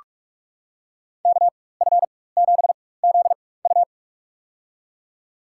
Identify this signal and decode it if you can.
Morse code, heard as a keyed tone, about 45 wpm, spaced as for 21 wpm, 700 Hz. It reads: KF7ZU
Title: USA Call Signs - 5 Characters - 4x Character Spacing - 45wpm